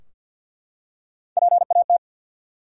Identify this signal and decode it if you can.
Morse code, heard as a keyed tone, about 52 wpm, 700 Hz. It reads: PAT
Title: USA Common Names - Max 3 Letters - 50wpm